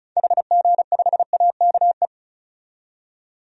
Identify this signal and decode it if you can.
Morse code, heard as a keyed tone, about 36 wpm, 700 Hz. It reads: HG5AKE
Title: DXCC-100 Call Signs - 6 Characters - 35wpm